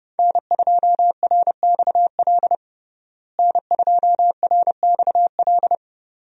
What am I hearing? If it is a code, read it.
Morse code, heard as a keyed tone, about 32 wpm, 700 Hz. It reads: N2RXL N2RXL
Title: USA Call Signs - 5 Characters - Repeated 2x - 30wpm